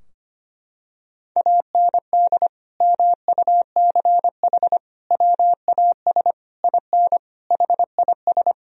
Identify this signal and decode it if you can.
Morse code, heard as a keyed tone, about 25 wpm, 700 Hz. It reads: AND MUCH WAS IN HIS
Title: Sentences from Top 1000 Words - Review Part 2 - 25wpm